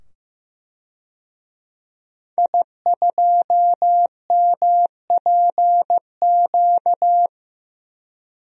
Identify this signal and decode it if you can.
Morse code, heard as a keyed tone, about 15 wpm, 700 Hz. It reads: I2MPQ
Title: DXCC-100 Call Signs - 5 Characters - 15wpm